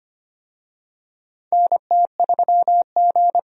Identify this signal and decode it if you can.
Morse code, heard as a keyed tone, about 25 wpm, 700 Hz. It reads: NT3G
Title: USA Call Signs - 4 Characters - Repeated 3x - 25wpm